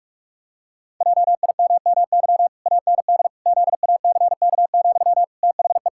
Morse code, heard as a keyed tone, about 41 wpm, 700 Hz. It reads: JIMMY AND ZACK, THE
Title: Pangram Challenge v2 - 45wpm